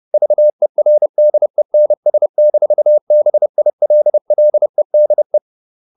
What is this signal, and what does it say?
Morse code, heard as a keyed tone, about 30 wpm, 600 Hz. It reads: VERDENS-BILLEDE
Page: Ferie 30 wpm